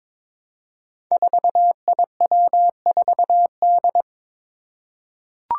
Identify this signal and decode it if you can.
Morse code, heard as a keyed tone, about 22 wpm, 700 Hz. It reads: 4IW4D
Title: Five Challenging Letter-Number Combinations 22wpm